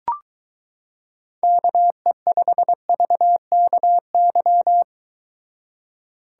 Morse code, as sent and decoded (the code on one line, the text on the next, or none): -.- . ..... ...- -.- -.--
KE5VKY